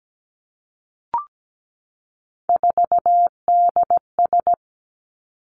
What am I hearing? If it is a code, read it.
Morse code, heard as a keyed tone, about 17 wpm, 700 Hz. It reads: 4DS
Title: Three Challenging Letter-Number Combinations 17wpm